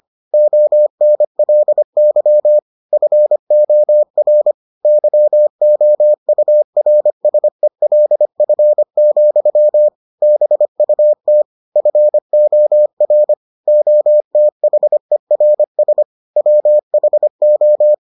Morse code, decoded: ONLY FOR YOURSELF, BUT FOR OTHERS WHO

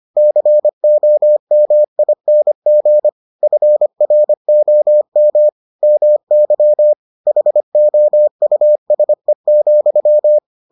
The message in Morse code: COMING FROM MY HOUSE,

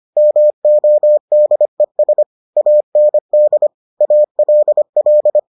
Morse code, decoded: MODES AND ALL

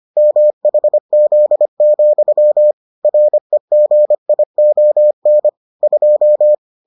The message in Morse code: MHZ, REGION 2

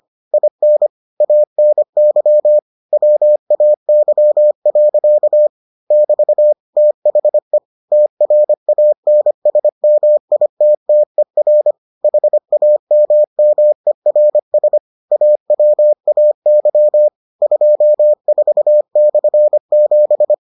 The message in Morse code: IN ANY WAY. <BT> THE TRANSMITTER HAMMERS AWAY 24/7

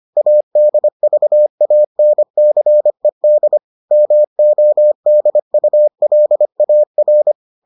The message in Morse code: ADVANCED MODULAR

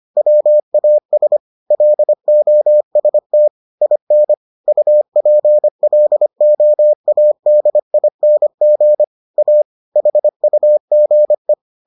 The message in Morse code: WAS LOST IN UPLOADING A HUGE

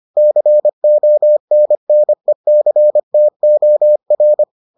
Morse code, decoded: CONNECTOR

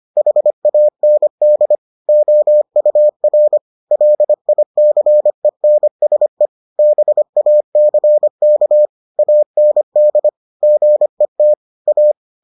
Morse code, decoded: HAND OUR LICENSE BACK AND GET A

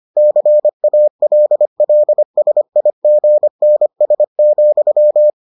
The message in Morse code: CALLSIGNS,

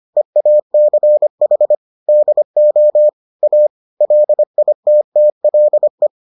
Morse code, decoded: EACH DO A LITTLE

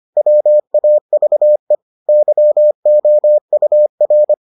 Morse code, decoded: WAVE YOUR

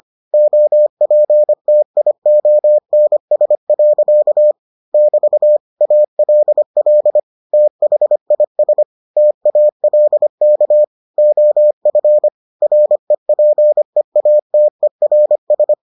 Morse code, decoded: OPTIONS. <BT> ALL THIS TALK OF REPEATERS

